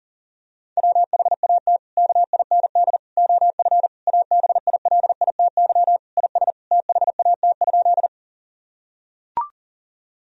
Morse code, heard as a keyed tone, about 40 wpm, 700 Hz. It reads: WHAT KIND OF ABILITY IS THAT?
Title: Sentences from Top 1000 Words - Review Part 3 - 40wpm